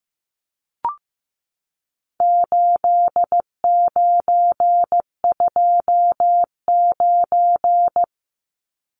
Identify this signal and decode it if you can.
Morse code, heard as a keyed tone, about 15 wpm, 700 Hz. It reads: 8929
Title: Numbers - 4-Digit 15wpm